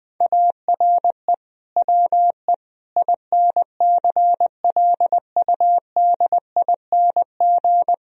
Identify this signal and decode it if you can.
Morse code, heard as a keyed tone, about 19 wpm, 700 Hz. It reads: ARE WE INCLUDING
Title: Sentences from Top 1000 Words - Review Part 4 - 20wpm